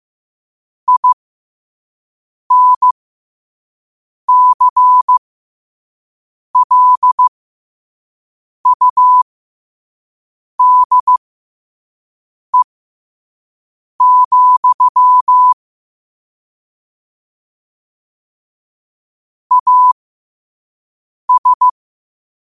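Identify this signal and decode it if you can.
Morse code, heard as a keyed tone, about 15 wpm, spaced as for 5 wpm, 1000 Hz. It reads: INCLUDE, AS
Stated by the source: Morse sent at 5 wpm.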